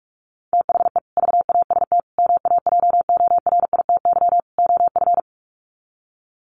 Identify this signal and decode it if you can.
Morse code, heard as a keyed tone, about 44 wpm, 700 Hz. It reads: THE VAST MAJORITY OF
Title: Sets of 4 Words 45wpm